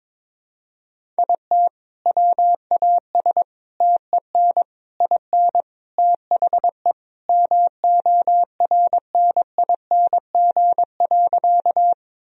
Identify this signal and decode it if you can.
Morse code, heard as a keyed tone, about 22 wpm, 700 Hz. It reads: IT WAS TEN IN THE MORNING.
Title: Sentences from Top 1000 Words - Review Part 2 - 22wpm